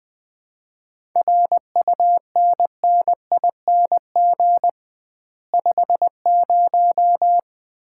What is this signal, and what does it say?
Morse code, heard as a keyed tone, about 20 wpm, 700 Hz. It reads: RUNNING 50
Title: Common QSO Phrases - 2x Word-Spacing - 20wpm